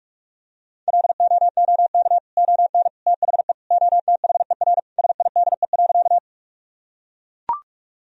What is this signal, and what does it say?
Morse code, heard as a keyed tone, about 41 wpm, 700 Hz. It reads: LOOK ON THE OTHER SIDE.
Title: Sentences from Top 1000 Words - Review Part 1 - 45wpm